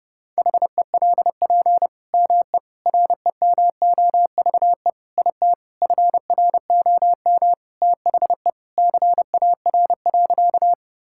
Morse code, heard as a keyed tone, about 32 wpm, 700 Hz. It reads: HELP ME REMOVE IT FROM THE CAR.
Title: Sentences from Top 1000 Words - Review Part 5 - 30wpm